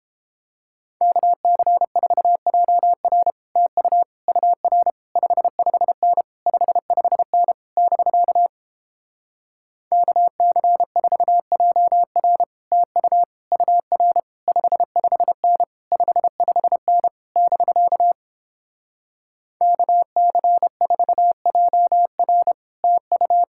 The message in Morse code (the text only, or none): KC4JR TU UR 55N 55N <BK> KC4JR TU UR 55N 55N <BK> KC4JR TU